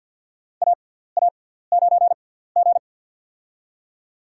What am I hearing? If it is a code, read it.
Morse code, heard as a keyed tone, about 49 wpm, spaced as for 17 wpm, 700 Hz. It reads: AA9G
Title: USA Call Signs - 6x Character Spacing - 50wpm